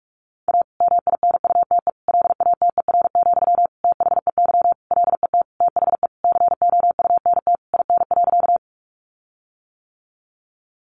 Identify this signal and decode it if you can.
Morse code, heard as a keyed tone, about 44 wpm, 700 Hz. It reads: A MINUTE LATER, THEY LET THE COUNT IN.
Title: Sentences from Top 1500 Words 45wpm